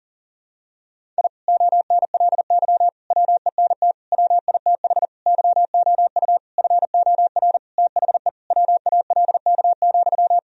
Morse code, heard as a keyed tone, about 37 wpm, 700 Hz. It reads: I ONLY WENT WITH YOU FOR THE WALK,
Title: Sentences from Top 1000 Words - Review Part 1 - 40wpm